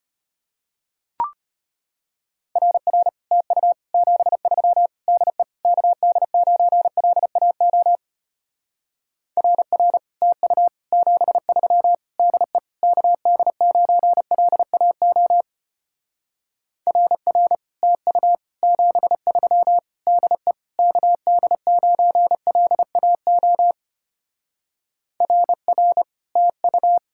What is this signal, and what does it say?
Morse code, heard as a keyed tone, about 33 wpm, 700 Hz. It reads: RR TU 73 DE KD9LAO RR TU 73 DE KD9LAO RR TU 73 DE KD9LAO RR TU